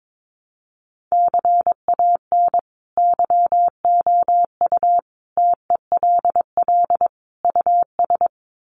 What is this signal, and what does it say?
Morse code, heard as a keyed tone, about 22 wpm, 700 Hz. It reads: CAN YOU TELL US
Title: Sets of 5 Words Encore2 22wpm